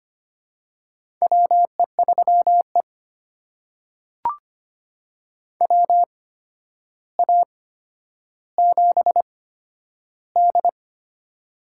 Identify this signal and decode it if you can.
Morse code, heard as a keyed tone, about 25 wpm, spaced as for 11 wpm, 700 Hz. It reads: WE3E WA7D
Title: USA Call Signs - 4 Characters - 8x Character Spacing - 25wpm